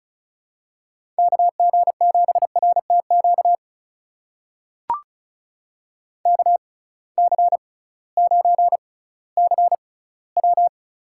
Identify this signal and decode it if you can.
Morse code, heard as a keyed tone, about 36 wpm, spaced as for 19 wpm, 700 Hz. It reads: KG7RTQ KC9CW
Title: USA Call Signs - 6x Character Spacing - 35wpm